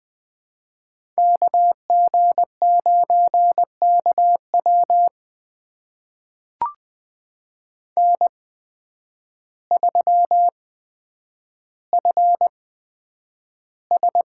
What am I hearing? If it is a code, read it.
Morse code, heard as a keyed tone, about 20 wpm, spaced as for 9 wpm, 700 Hz. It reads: KG9KW N3FS
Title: USA Call Signs - 5 Characters - 8x Character Spacing - 20wpm